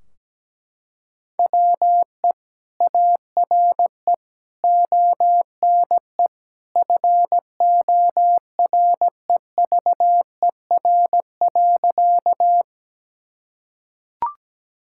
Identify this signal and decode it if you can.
Morse code, heard as a keyed tone, about 17 wpm, 700 Hz. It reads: WE ARE ONE FOREVER.
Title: Sentences from Top 1600 Words 17wpm